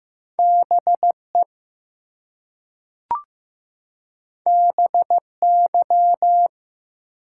Morse code, heard as a keyed tone, about 15 wpm, 700 Hz. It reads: BE BY
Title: Two Letter Words - 15wpm